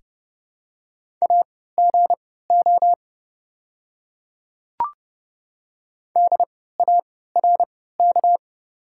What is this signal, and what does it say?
Morse code, heard as a keyed tone, about 30 wpm, spaced as for 17 wpm, 700 Hz. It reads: AGO DARK
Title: 3x Character Spacing - Top 500 Words - 30wpm